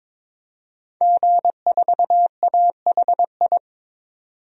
Morse code, heard as a keyed tone, about 22 wpm, 700 Hz. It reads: G4AHI
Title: DXCC-100 Call Signs - 5 Characters - 22wpm